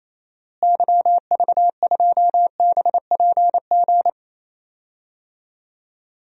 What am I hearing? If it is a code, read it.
Morse code, heard as a keyed tone, about 28 wpm, 700 Hz. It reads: YV2BPG
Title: DXCC-100 Call Signs - 6 Characters - Repeated 2x - 28wpm